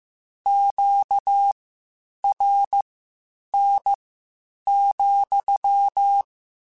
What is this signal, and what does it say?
Morse code, heard as a keyed tone, about 15 wpm, spaced as for 8 wpm, 800 Hz. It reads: QRN,
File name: Q Calls / Q Calls-mp3s / Q CALLS SPSL @13